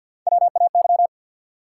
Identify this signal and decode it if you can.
Morse code, heard as a keyed tone, about 50 wpm, 700 Hz. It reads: WAY
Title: Sets of 2 Words - New Word Drill - 50wpm